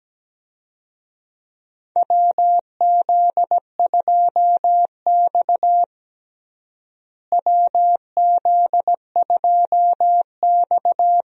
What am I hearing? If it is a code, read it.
Morse code, heard as a keyed tone, about 17 wpm, 700 Hz. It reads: WZ2X WZ2X
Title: USA Call Signs - 4 Characters - Repeated 2x - 17wpm